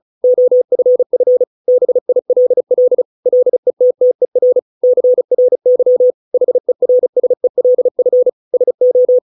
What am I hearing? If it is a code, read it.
Morse code, heard as a keyed tone, about 34 wpm, 500 Hz. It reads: OFF BILL LETTER CRY HERSELF SO